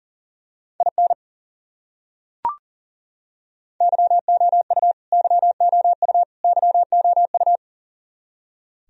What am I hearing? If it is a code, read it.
Morse code, heard as a keyed tone, about 40 wpm, 700 Hz. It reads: IN YOU YOU YOU
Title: Top 100 Words Repeated 40wpm